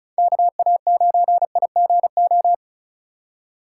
Morse code, heard as a keyed tone, about 44 wpm, 700 Hz. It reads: KA9IGO